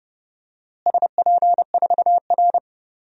Morse code, heard as a keyed tone, about 32 wpm, 700 Hz. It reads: SP4R
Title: DXCC-100 Call Signs - 4 Characters - Repeated 2x - 30wpm